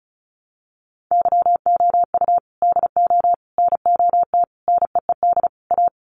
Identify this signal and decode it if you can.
Morse code, heard as a keyed tone, about 34 wpm, 700 Hz. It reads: YOU DO NOT NEED A
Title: Sets of 5 Words Encore 35wpm